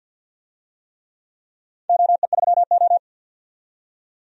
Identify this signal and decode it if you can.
Morse code, heard as a keyed tone, about 47 wpm, 700 Hz. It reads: OE3O
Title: DXCC-100 Call Signs - 4 Characters - 50wpm